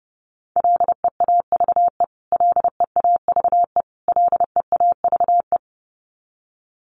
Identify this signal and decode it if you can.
Morse code, heard as a keyed tone, about 30 wpm, 700 Hz. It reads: LEAVE LEAVE LEAVE
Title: Top 500 Words Repeated 30wpm